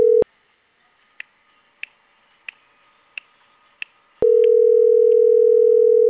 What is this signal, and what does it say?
Ringback tone (North America), 440 + 480 Hz.